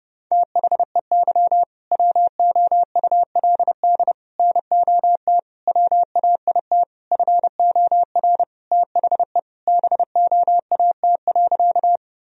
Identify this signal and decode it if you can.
Morse code, heard as a keyed tone, about 30 wpm, 700 Hz. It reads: THEY WOULD NOT WAIT FOR THE BOAT.
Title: Sentences from Top 1000 Words - Review Part 3 - 30wpm